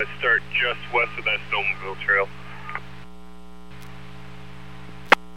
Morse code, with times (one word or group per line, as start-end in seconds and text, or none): none